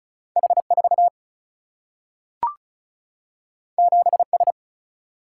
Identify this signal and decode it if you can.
Morse code, heard as a keyed tone, about 36 wpm, 700 Hz. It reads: H4 7S
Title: Single-Letter Single-Number 35wpm